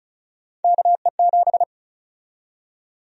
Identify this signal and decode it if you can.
Morse code, heard as a keyed tone, about 36 wpm, 700 Hz. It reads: KE7
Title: USA Call Sign Prefixes - 35wpm